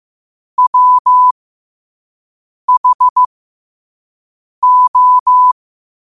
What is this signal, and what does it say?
Morse code, heard as a keyed tone, about 15 wpm, spaced as for 5 wpm, 1000 Hz. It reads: WHO
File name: qotdfiles / 5wpm